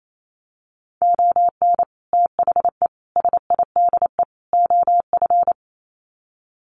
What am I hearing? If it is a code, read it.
Morse code, heard as a keyed tone, about 28 wpm, 700 Hz. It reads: ON THE SIDE OF
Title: Sets of 4 Words 28wpm